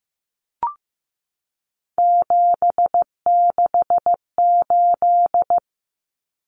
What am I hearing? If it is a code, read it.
Morse code, heard as a keyed tone, about 15 wpm, 700 Hz. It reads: 768
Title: Common Cut Numbers - 3-Digit 15wpm